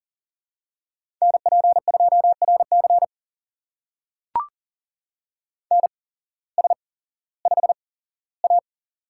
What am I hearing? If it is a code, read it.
Morse code, heard as a keyed tone, about 42 wpm, spaced as for 19 wpm, 700 Hz. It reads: NP2RC NS5A
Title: USA Call Signs - 8x Character Spacing - 40wpm